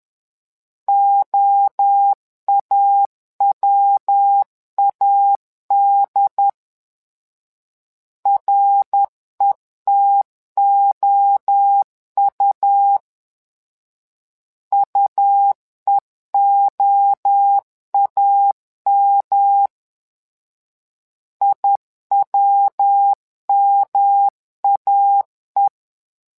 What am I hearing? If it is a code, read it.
Morse code, heard as a keyed tone, about 10 wpm, 800 Hz. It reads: OAWAD RETOU UEOAM IWMAE
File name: LEZIONI / 3 MWRO / 3 MWRO 10wpm 2148